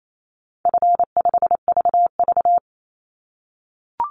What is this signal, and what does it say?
Morse code, heard as a keyed tone, about 28 wpm, 700 Hz. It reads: F5VV
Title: Easy Call Signs - 28wpm